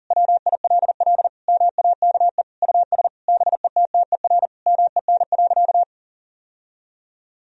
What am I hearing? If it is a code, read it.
Morse code, heard as a keyed tone, about 40 wpm, 700 Hz. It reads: WILL MAKE US BETTER MEN.
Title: Sentences from Top 1000 Words - Review Part 3 - 40wpm